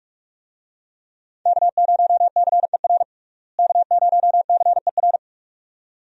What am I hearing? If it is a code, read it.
Morse code, heard as a keyed tone, about 41 wpm, 700 Hz. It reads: K0CER K0CER
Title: USA Call Signs - 5 Characters - Repeated 2x - 45wpm